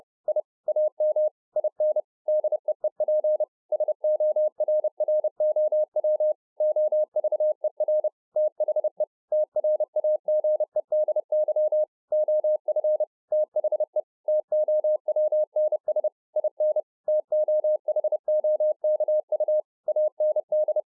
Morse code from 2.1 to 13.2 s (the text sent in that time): DEEP SORROW OVER THE TRAGEDY OF